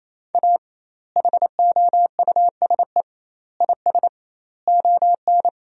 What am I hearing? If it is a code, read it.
Morse code, heard as a keyed tone, about 28 wpm, 700 Hz. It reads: A HOUSE IS ON
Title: 2x Word-Spacing - Sentences from Top 400 Words - 28wpm